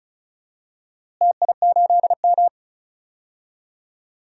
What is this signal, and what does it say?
Morse code, heard as a keyed tone, about 34 wpm, 700 Hz. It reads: TI8M
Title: DXCC-100 Call Signs - 4 Characters - Repeated 3x - 35wpm